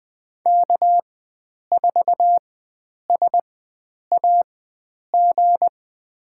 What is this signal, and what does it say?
Morse code, heard as a keyed tone, about 20 wpm, spaced as for 9 wpm, 700 Hz. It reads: K4SAG
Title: USA Call Signs - 5 Characters - 4x Character Spacing - 20wpm